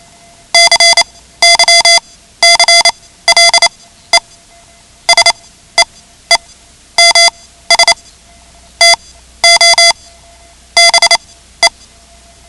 Morse code, heard as a keyed tone, about 28 wpm, spaced as for 14 wpm, 2050 Hz. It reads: CYCLE SEEMS TO BE
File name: ham radio / CodeQuick / Code Quick 10